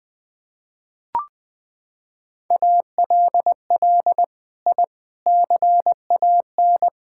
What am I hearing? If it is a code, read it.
Morse code, heard as a keyed tone, about 21 wpm, 700 Hz. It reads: ALL I CAN
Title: Sentences with 6-Word Phrases - 20wpm